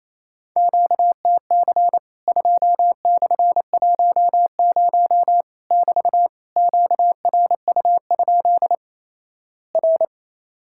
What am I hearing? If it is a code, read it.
Morse code, heard as a keyed tone, about 28 wpm, 700 Hz. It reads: QTC 2/10 <BT> QRU? R
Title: Worked All Europe DX Contest - QTC Exchange - 28wpm